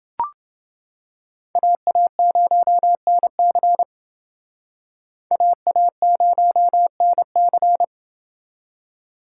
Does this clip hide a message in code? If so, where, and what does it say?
Yes: Morse code from 1.6 to 7.8 s, heard as a keyed tone, about 28 wpm, 700 Hz. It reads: AA0NC AA0NC